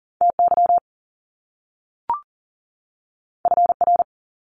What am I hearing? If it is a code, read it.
Morse code, heard as a keyed tone, about 40 wpm, 700 Hz. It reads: TY FR
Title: Top Two Letter Combinations in Words 40wpm